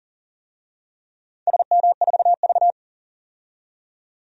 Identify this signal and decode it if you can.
Morse code, heard as a keyed tone, about 38 wpm, 700 Hz. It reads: SM4V
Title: DXCC-100 Call Signs - 4 Characters - 40wpm